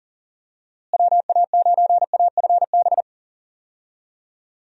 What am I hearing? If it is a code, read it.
Morse code, heard as a keyed tone, about 37 wpm, 700 Hz. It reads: WA9AFB
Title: USA Call Signs - 6 Characters - Repeated 3x - 40wpm